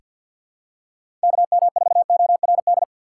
Morse code, heard as a keyed tone, about 47 wpm, 700 Hz. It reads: KM4ORD
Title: USA Call Signs - 4x Character Spacing - 50wpm